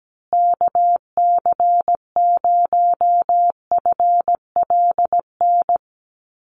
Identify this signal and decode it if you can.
Morse code, heard as a keyed tone, about 17 wpm, 700 Hz. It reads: KC0FLN
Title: USA Call Signs Repeated 3x 17wpm